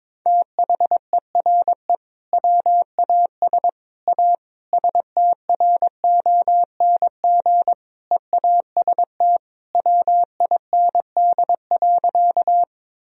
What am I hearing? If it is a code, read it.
Morse code, heard as a keyed tone, about 21 wpm, 700 Hz. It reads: THERE WAS A STRONG EAST WIND.
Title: Sentences from Top 1000 Words - Review Part 3 - 22wpm